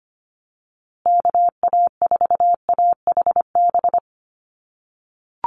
Morse code, heard as a keyed tone, about 25 wpm, 700 Hz. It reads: KA4AHB
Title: USA Call Signs Repeated 2x 25wpm